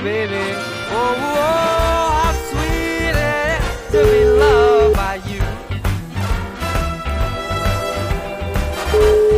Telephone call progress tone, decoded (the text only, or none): ringback tone